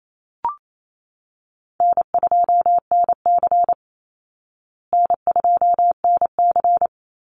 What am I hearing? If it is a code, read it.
Morse code, heard as a keyed tone, about 28 wpm, 700 Hz. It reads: N2NC N2NC
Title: Call Signs Repeated 3x - 28wpm